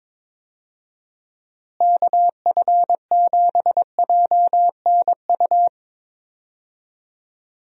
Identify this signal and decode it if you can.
Morse code, heard as a keyed tone, about 22 wpm, 700 Hz. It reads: KF7JNU
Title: USA Call Signs - 6 Characters - Repeated 2x - 22wpm